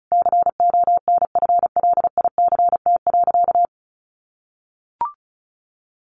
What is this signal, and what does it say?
Morse code, heard as a keyed tone, about 34 wpm, 700 Hz. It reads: CONFLICT.
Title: Sentences from Top 1300 Words 35wpm